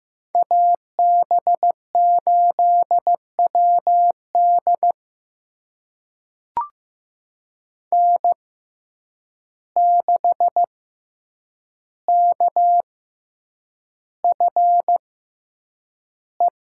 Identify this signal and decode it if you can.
Morse code, heard as a keyed tone, about 15 wpm, spaced as for 8 wpm, 700 Hz. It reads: AB8WD N6KFE
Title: USA Call Signs - 6x Character Spacing - 15wpm